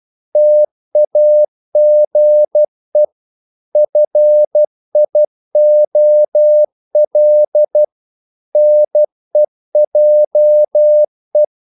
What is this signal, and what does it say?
Morse code, heard as a keyed tone, about 12 wpm, 600 Hz. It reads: TAGE FIOL NEJE
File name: Korte ord DK 12wpm